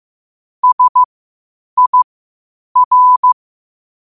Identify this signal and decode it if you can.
Morse code, heard as a keyed tone, about 15 wpm, spaced as for 8 wpm, 1000 Hz. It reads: SIR
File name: qotdfiles / 8wpm